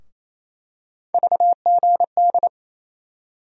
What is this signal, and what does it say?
Morse code, heard as a keyed tone, about 28 wpm, 700 Hz. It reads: VGD
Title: USA Call Sign Postfixes - 3 Letters 28wpm